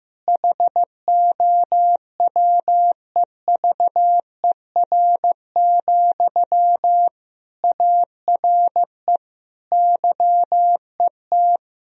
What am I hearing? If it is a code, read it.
Morse code, heard as a keyed tone, about 16 wpm, 700 Hz. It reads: HOWEVER, ARE YET
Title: Sentences from Top 1000 Words - Review Part 1 - 15wpm